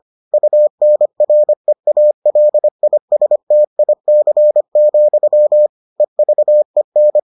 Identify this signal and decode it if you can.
Morse code, heard as a keyed tone, about 25 wpm, 600 Hz. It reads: UNREALISTIC, EVEN